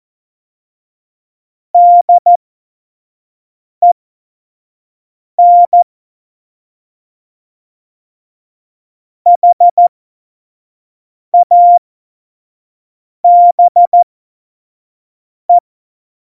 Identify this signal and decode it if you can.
Morse code, heard as a keyed tone, about 14 wpm, spaced as for 5 wpm, 700 Hz. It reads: DEN HABE
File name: Morsekurs / Witze / Witz 15. September 2020, 25 F70